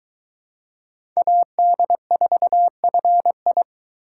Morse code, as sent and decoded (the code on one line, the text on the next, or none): .- -.. ....- ..-. ..
AD4FI